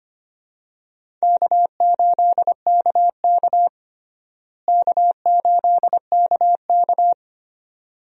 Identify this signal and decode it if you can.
Morse code, heard as a keyed tone, about 25 wpm, 700 Hz. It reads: K8KK K8KK
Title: USA Call Signs - 4 Characters - Repeated 2x - 25wpm